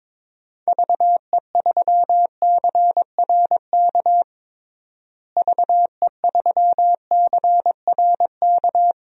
Morse code, heard as a keyed tone, about 22 wpm, 700 Hz. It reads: VE3CRK VE3CRK
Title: DXCC-100 Call Signs - 6 Characters - Repeated 3x - 22wpm